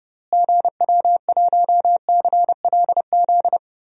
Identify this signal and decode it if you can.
Morse code, heard as a keyed tone, about 30 wpm, 700 Hz. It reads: GW1CLZ
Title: DXCC-100 Call Signs - 6 Characters - Repeated 3x - 30wpm